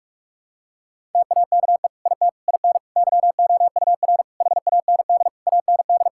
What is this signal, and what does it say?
Morse code, heard as a keyed tone, about 46 wpm, 700 Hz. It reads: TAKE IT IN YOUR HAND AND
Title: Sentences from Top 1600 Words 45wpm